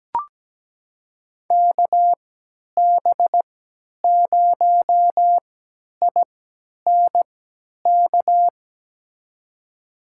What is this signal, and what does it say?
Morse code, heard as a keyed tone, about 17 wpm, spaced as for 10 wpm, 700 Hz. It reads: KB0INK
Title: USA Call Signs - 3x Character Spacing - 17wpm